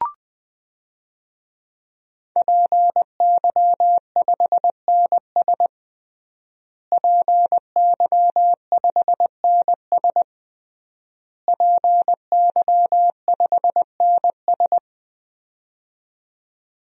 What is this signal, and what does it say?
Morse code, heard as a keyed tone, about 21 wpm, 700 Hz. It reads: PY5NS PY5NS PY5NS
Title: DXCC-100 Call Signs - 5 Characters - Repeated 3x - 20wpm